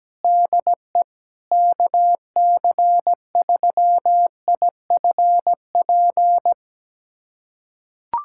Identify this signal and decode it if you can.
Morse code, heard as a keyed tone, about 17 wpm, 700 Hz. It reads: DE KC3IFP
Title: USA Callsign DE Callsign 17wpm